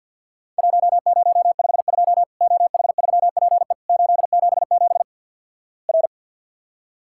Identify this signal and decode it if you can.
Morse code, heard as a keyed tone, about 50 wpm, 700 Hz. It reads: 1052 OH3PE 877 R
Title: Worked All Europe DX Contest - QTC Exchange with Repeats - 50wpm